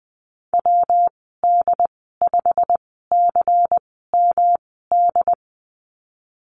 Morse code, heard as a keyed tone, about 19 wpm, spaced as for 14 wpm, 700 Hz. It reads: WD5CMD
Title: 140 Lesson 23g - Letter f - Call Signs 20wpm